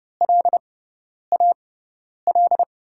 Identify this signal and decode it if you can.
Morse code, heard as a keyed tone, about 30 wpm, spaced as for 10 wpm, 700 Hz. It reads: LAL
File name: Lesson 18g - Letter m - Call Signs 30wpm